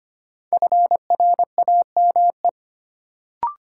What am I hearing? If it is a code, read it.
Morse code, heard as a keyed tone, about 25 wpm, 700 Hz. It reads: FRAME
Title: Sentences from Top 1000 Words - Review Part 6 - New Word Drill - 25wpm